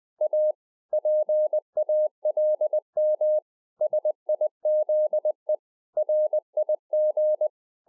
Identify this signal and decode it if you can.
Morse code, heard as a keyed tone, about 20 wpm, 610 Hz. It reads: A PALM SIZE RIG
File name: 2011-04-mp3 / A SMALL TRANCEIVER 3 20WPM